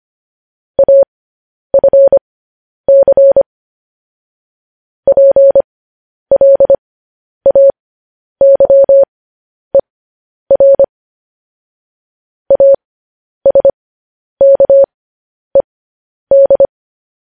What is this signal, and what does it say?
Morse code, heard as a keyed tone, about 25 wpm, spaced as for 10 wpm, 550 Hz. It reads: AFC PLAYER ASKED
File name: podcast / News headlines-1x-25x10